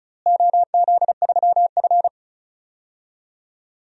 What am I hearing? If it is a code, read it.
Morse code, heard as a keyed tone, about 36 wpm, 690 Hz. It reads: OZ3F
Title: DXCC-100 Call Signs - 4 Characters - Repeated 3x - 35wpm